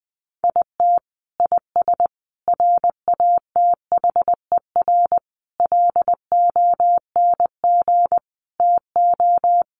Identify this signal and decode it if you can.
Morse code, heard as a keyed tone, about 19 wpm, 700 Hz. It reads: IT IS RATHER LONG TO
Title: Sentences from Top 700 Words 20wpm